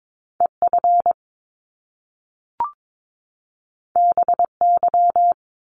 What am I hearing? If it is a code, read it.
Morse code, heard as a keyed tone, about 22 wpm, 700 Hz. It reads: EF BY
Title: Top Two Letter Combinations in Words 22wpm